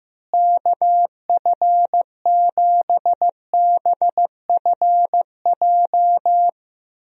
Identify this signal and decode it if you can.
Morse code, heard as a keyed tone, about 15 wpm, 700 Hz. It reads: KF7BFJ
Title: USA Call Signs - 6 Characters - Repeated 2x - 15wpm